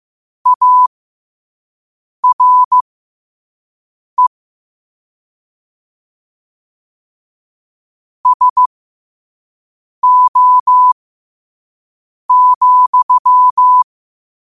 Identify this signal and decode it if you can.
Morse code, heard as a keyed tone, about 15 wpm, spaced as for 5 wpm, 1000 Hz. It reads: ARE SO,
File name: 5wpm